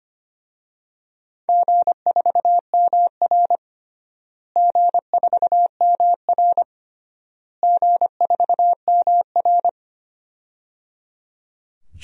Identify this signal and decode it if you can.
Morse code, heard as a keyed tone, about 25 wpm, 700 Hz. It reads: G4MR G4MR G4MR
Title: DXCC-100 Call Signs - 4 Characters - Repeated 3x - 25wpm